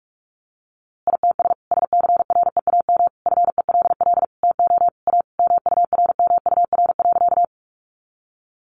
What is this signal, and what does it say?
Morse code, heard as a keyed tone, about 44 wpm, 700 Hz. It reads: ITS SCREAM FELL TO A MURMUR.
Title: A Sound Of Thunder 45wpm